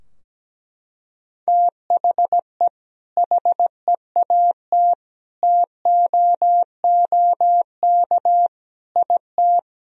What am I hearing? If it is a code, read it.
Morse code, heard as a keyed tone, about 17 wpm, 700 Hz. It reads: THE HEAT TOOK IT
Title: Sentences from Top 1000 Words - Review Part 3 - 17wpm